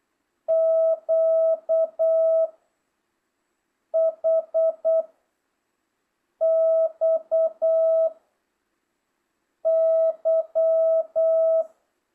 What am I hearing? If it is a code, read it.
Morse code, heard as a keyed tone, about 8 wpm, 660 Hz. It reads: Q H X Y